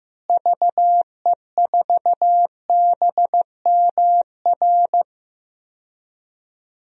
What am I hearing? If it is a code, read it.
Morse code, heard as a keyed tone, about 15 wpm, 700 Hz. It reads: VE4BMR
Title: DXCC-100 Call Signs - 6 Characters - Repeated 3x - 15wpm